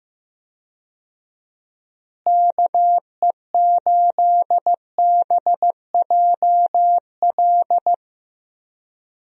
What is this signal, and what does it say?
Morse code, heard as a keyed tone, about 15 wpm, 700 Hz. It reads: KE8BJL
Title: USA Call Signs - 6 Characters - Repeated 2x - 15wpm